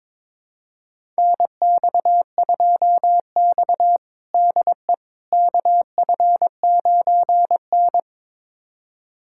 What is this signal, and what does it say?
Morse code, heard as a keyed tone, about 21 wpm, 700 Hz. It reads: NX2X DE KF9N
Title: USA Callsign DE Callsign - 4 Characters - 22wpm